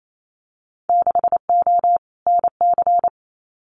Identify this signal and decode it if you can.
Morse code, heard as a keyed tone, about 28 wpm, 700 Hz. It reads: 6O NC
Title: Winter Field Day Exchanges 28wpm